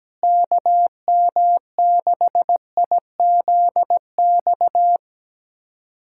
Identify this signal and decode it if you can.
Morse code, heard as a keyed tone, about 17 wpm, 700 Hz. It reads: KM6IZX
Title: USA Callsign DE Callsign - 6 Characters - 17wpm